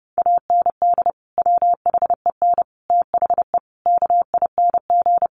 Morse code, heard as a keyed tone, about 30 wpm, 700 Hz. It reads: AND WHEN THE KING
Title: Sentences from Top 1100 Words 30wpm